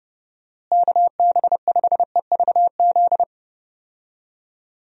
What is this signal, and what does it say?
Morse code, heard as a keyed tone, about 30 wpm, 700 Hz. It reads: KB5EVZ
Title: USA Call Signs - 6 Characters - Repeated 2x - 30wpm